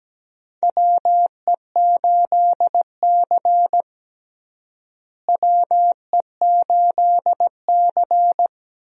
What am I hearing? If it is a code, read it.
Morse code, heard as a keyed tone, about 17 wpm, 700 Hz. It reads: WE8C WE8C
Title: USA Call Signs - 4 Characters - Repeated 2x - 17wpm